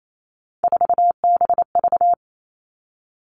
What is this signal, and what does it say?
Morse code, heard as a keyed tone, about 28 wpm, 700 Hz. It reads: 4BV
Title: Three Challenging Letter-Number Combinations 28wpm